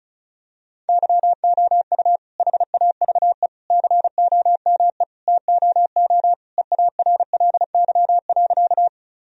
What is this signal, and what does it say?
Morse code, heard as a keyed tone, about 36 wpm, 700 Hz. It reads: YOU HAVE COME TOO EARLY.
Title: Sentences from Top 1000 Words - Review Part 2 - 35wpm